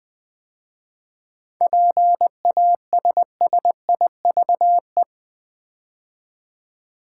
Morse code, heard as a keyed tone, about 20 wpm, 700 Hz. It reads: PASSIVE
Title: 7 Letter Words - First 3 Letters Given - 20wpm